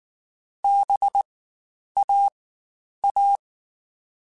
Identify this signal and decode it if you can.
Morse code, heard as a keyed tone, about 19 wpm, spaced as for 9 wpm, 800 Hz. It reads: BAA
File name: Spaced Alphas / Spaced Alphas mp3s / Bs sp @18